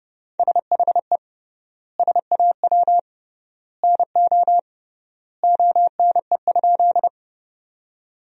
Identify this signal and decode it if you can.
Morse code, heard as a keyed tone, about 28 wpm, 700 Hz. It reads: SHE SAW NO ONE?
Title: 3x Word-Spacing - Sentences from Top 300 Words - 30wpm